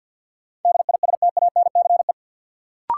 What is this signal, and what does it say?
Morse code, heard as a keyed tone, about 51 wpm, 700 Hz. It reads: DISTANCE
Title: Sentences from Top 1200 Words - New Word Drill Alt - 50wpm